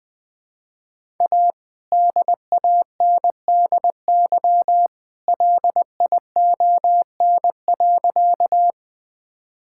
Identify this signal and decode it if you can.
Morse code, heard as a keyed tone, about 20 wpm, 700 Hz. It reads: A DANDY LION.
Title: All-Age Appropriate Jokes - 20wpm